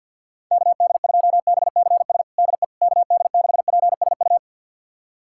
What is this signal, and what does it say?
Morse code, heard as a keyed tone, about 52 wpm, 690 Hz. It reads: KD2BCS DE KD6PSU
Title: USA Callsign DE Callsign 50wpm